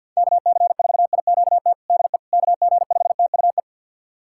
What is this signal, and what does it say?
Morse code, heard as a keyed tone, about 45 wpm, 700 Hz. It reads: KC4IQT DE KG5TUE
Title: USA Callsign DE Callsign 50wpm